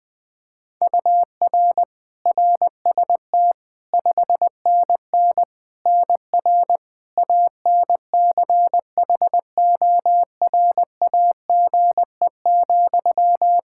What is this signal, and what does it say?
Morse code, heard as a keyed tone, about 21 wpm, 700 Hz. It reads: UR RST 5NN NR ANCHORAGE,
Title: Practice QSO - Exchange 1 - Beginning - 20wpm